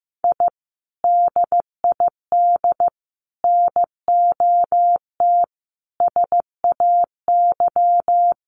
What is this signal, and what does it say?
Morse code, heard as a keyed tone, about 15 wpm, 700 Hz. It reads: I DID NOT SAY
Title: Sets of 5 Words Encore2 15wpm